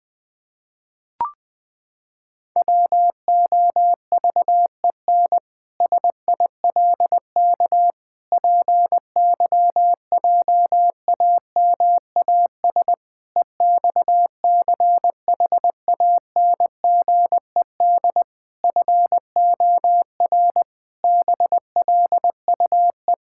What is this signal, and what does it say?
Morse code, heard as a keyed tone, about 20 wpm, 700 Hz. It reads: WOVEN SILK PYJAMAS EXCHANGED FOR BLUE
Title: Pangram Challenge v2 - 20wpm